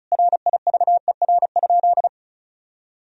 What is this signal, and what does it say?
Morse code, heard as a keyed tone, about 36 wpm, 700 Hz. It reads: RIVER?
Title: Sentences from Top 1000 Words - Review Part 4 - 35wpm